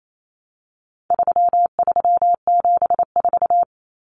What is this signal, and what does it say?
Morse code, heard as a keyed tone, about 28 wpm, 700 Hz. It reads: 3374
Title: Numbers - 4-Digit 28wpm